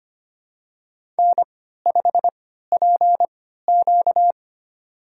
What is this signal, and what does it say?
Morse code, heard as a keyed tone, about 25 wpm, spaced as for 14 wpm, 700 Hz. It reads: N5PQ
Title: USA Call Signs - 3x Character Spacing - 25wpm